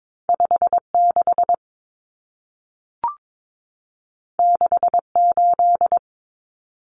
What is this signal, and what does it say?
Morse code, heard as a keyed tone, about 22 wpm, 700 Hz. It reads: 56 68
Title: Numbers - 2-Digit 22wpm